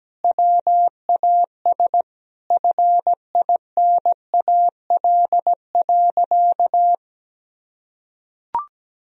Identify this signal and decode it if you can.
Morse code, heard as a keyed tone, about 17 wpm, 700 Hz. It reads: WAS FINAL.
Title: Sentences from Top 1000 Words - Review Part 2 - 17wpm